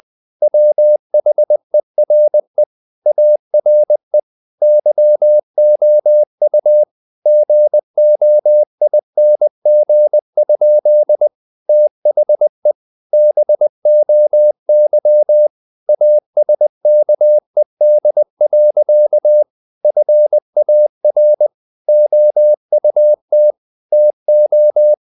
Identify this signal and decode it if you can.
Morse code, heard as a keyed tone, about 20 wpm, 600 Hz. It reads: WHERE ARE YOU GOING? THE BOY ASKED. FAR OUT TO
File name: the old man and the sea / the old man and the sea part 1 20wpm